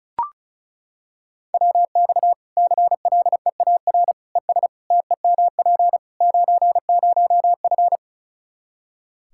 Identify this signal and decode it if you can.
Morse code, heard as a keyed tone, about 34 wpm, 700 Hz. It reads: WX CLEAR ES TEMP 90F
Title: Practice QSO - Exchange 2 - Info - 35wpm